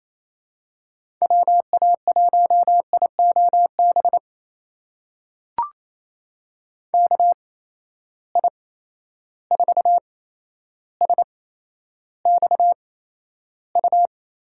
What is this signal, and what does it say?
Morse code, heard as a keyed tone, about 28 wpm, spaced as for 12 wpm, 700 Hz. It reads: WA1IOB KI4SXU
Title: USA Call Signs - 6 Characters - 8x Character Spacing - 28wpm